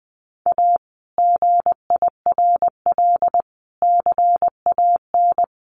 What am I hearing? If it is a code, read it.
Morse code, heard as a keyed tone, about 20 wpm, 700 Hz. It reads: A GIRL CAN
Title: Sentences from Top 1400 Words 20wpm